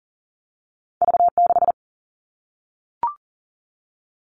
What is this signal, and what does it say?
Morse code, heard as a keyed tone, about 40 wpm, 700 Hz. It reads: V6
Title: All Cut Numbers - 2-Digit 40wpm